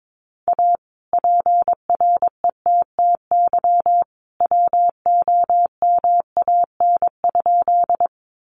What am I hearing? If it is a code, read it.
Morse code, heard as a keyed tone, about 22 wpm, 700 Hz. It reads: A PRETTY WOMAN?
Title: Sentences from Top 1100 Words 22wpm